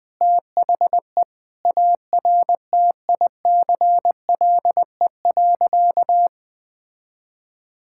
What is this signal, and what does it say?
Morse code, heard as a keyed tone, about 20 wpm, 700 Hz. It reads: THE ARTICLE.
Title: Sentences from Top 1000 Words - Review Part 5 - 20wpm